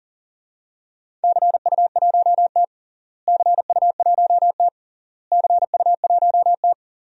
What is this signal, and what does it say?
Morse code, heard as a keyed tone, about 40 wpm, 700 Hz. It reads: CU1T CU1T CU1T
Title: DXCC-100 Call Signs - 4 Characters - Repeated 3x - 40wpm